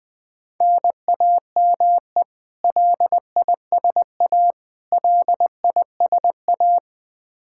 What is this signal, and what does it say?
Morse code, heard as a keyed tone, about 21 wpm, 700 Hz. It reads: NAME LISA LISA
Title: Practice QSO - Exchange 1 - Beginning - 20wpm